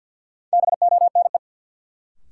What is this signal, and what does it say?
Morse code, heard as a keyed tone, about 48 wpm, 700 Hz. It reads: BONE
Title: Sentences from Top 1600 Words - New Word Drill Alt - 50wpm